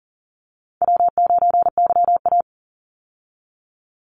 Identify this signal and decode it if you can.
Morse code, heard as a keyed tone, about 40 wpm, 700 Hz. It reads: W9YA
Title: Easy Call Signs Repeated 3x - 40wpm